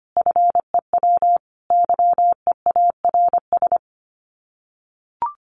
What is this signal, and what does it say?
Morse code, heard as a keyed tone, about 25 wpm, 690 Hz. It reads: FEW YEARS
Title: Sets of 5 Words Encore 25wpm